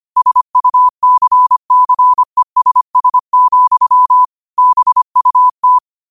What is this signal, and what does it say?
Morse code, heard as a keyed tone, about 25 wpm, 1000 Hz. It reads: SUCCESS, BUT
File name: 25wpm